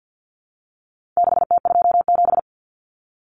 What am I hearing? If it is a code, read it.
Morse code, heard as a keyed tone, about 50 wpm, 700 Hz. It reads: 6T27
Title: Common Cut Numbers - 4-Digit 50wpm